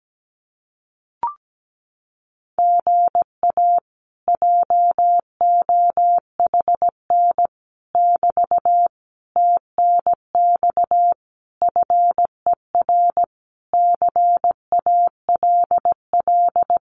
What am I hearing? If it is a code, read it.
Morse code, heard as a keyed tone, about 17 wpm, 700 Hz. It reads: GA JOHN <BT> TNX FER CALL
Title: Common QSO Phrases Encore 17wpm